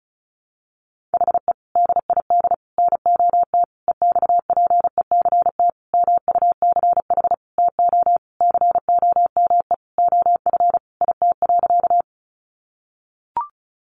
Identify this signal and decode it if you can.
Morse code, heard as a keyed tone, about 34 wpm, 700 Hz. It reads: HE DID NOT EXPECT MUCH TO COME OF IT.
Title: Sentences from Top 1100 Words 35wpm